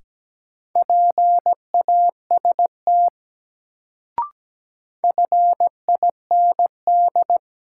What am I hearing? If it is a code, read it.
Morse code, heard as a keyed tone, about 17 wpm, 700 Hz. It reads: PAST FIND
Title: Sets of 2 Words - New Word Drill - 17wpm